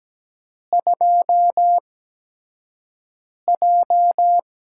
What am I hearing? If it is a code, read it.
Morse code, heard as a keyed tone, about 17 wpm, spaced as for 5 wpm, 700 Hz. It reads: 2J
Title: USA Call Signs - 4 Characters - 8x Character Spacing - 17wpm